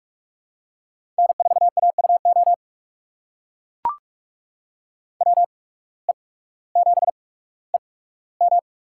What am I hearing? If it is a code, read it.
Morse code, heard as a keyed tone, about 44 wpm, spaced as for 19 wpm, 700 Hz. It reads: N4AUO WE7EM
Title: USA Call Signs - 5 Characters - 8x Character Spacing - 45wpm